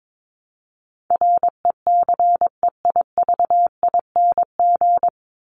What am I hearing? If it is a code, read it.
Morse code, heard as a keyed tone, about 22 wpm, 700 Hz. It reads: RECEIVING
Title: ING Suffix - Top 100 Words with Base Words 22wpm